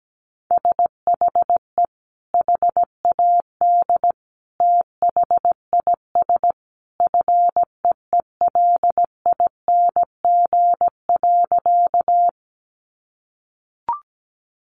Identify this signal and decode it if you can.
Morse code, heard as a keyed tone, about 17 wpm, 700 Hz. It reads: SHE HAD THIS FEELING.
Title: Sentences from Top 1200 Words 17wpm